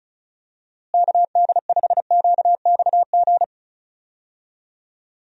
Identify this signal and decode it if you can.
Morse code, heard as a keyed tone, about 34 wpm, 700 Hz. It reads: KD5QXG
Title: USA Call Signs - 6 Characters - Repeated 3x - 35wpm